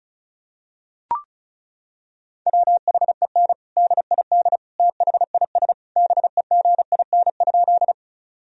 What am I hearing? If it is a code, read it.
Morse code, heard as a keyed tone, about 34 wpm, 700 Hz. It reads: WHEN DID THIS BEGIN?
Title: Sentences from Top 1000 Words - Review Part 1 - 35wpm